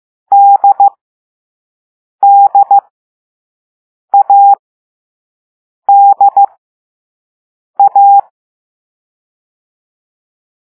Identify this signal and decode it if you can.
Morse code, heard as a keyed tone, about 15 wpm, spaced as for 5 wpm, 800 Hz. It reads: DDADA
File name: Morse / Level3B